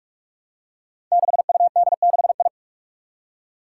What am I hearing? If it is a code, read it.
Morse code, heard as a keyed tone, about 44 wpm, 700 Hz. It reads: 6UD6I
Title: Five Challenging Letter-Number Combinations 45wpm